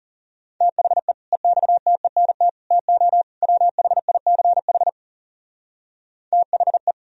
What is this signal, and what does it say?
Morse code, heard as a keyed tone, about 37 wpm, 700 Hz. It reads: THE EXTENT TO WHICH THE
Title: Speed-Racing - Sets of 4 Words 30wpm